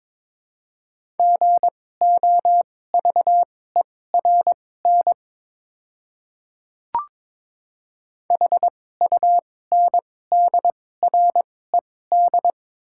Morse code, heard as a keyed tone, about 22 wpm, spaced as for 16 wpm, 700 Hz. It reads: GOVERN HUNDRED
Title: 2x Character Spacing - Top 500 Words - 22wpm